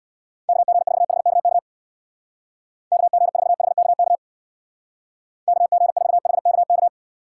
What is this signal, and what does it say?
Morse code, heard as a keyed tone, about 65 wpm, 700 Hz. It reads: KG4HKK KG4HKK KG4HKK